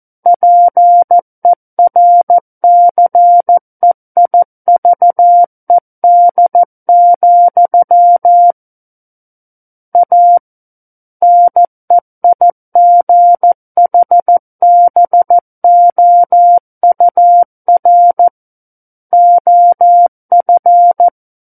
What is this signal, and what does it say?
Morse code, heard as a keyed tone, about 14 wpm, 700 Hz. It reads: PERCEIVED, A NEIGHBOUR OF
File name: Book One / 05-Chapter 5 - 14 WPM